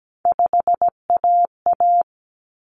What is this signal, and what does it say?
Morse code, heard as a keyed tone, about 17 wpm, 690 Hz. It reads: 5AA
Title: Common Cut Numbers - 3-Digit 17wpm